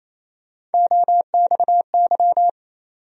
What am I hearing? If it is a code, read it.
Morse code, heard as a keyed tone, about 28 wpm, 700 Hz. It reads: OXY
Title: USA Call Sign Postfixes - 3 Letters 28wpm